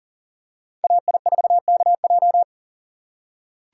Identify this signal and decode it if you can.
Morse code, heard as a keyed tone, about 37 wpm, 700 Hz. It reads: AI4KJ
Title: USA Call Signs - 5 Characters - 4x Character Spacing - 40wpm